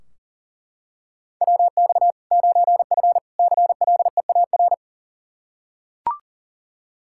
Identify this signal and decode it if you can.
Morse code, heard as a keyed tone, about 40 wpm, 700 Hz. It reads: WX 9F CLEAR
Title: K1USN SST QSO Practice 40wpm